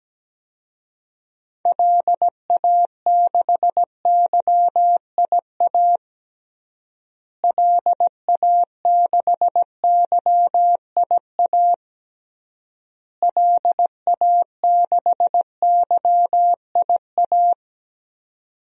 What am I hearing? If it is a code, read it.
Morse code, heard as a keyed tone, about 17 wpm, 700 Hz. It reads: LA6YIA LA6YIA LA6YIA
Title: DXCC-100 Call Signs - 6 Characters - Repeated 3x - 17wpm